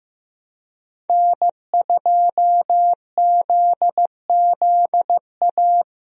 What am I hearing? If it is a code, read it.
Morse code, heard as a keyed tone, about 15 wpm, 700 Hz. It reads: N2ZZA
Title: USA Call Signs - 5 Characters - Repeated 2x - 15wpm